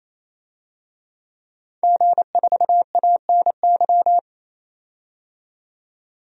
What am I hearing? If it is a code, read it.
Morse code, heard as a keyed tone, about 28 wpm, 700 Hz. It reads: G4ANY
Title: DXCC-100 Call Signs - 5 Characters - 28wpm